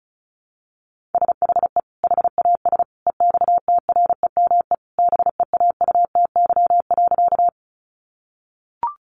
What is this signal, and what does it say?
Morse code, heard as a keyed tone, about 34 wpm, 700 Hz. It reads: SHE HAS EXTREME BEAUTY.
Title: Sentences from Top 1300 Words 35wpm